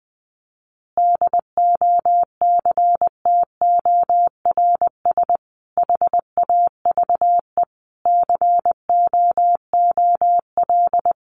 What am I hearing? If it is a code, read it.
Morse code, heard as a keyed tone, about 20 wpm, 700 Hz. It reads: DOCTORS HAVE COOL
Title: Easy Phrases 20wpm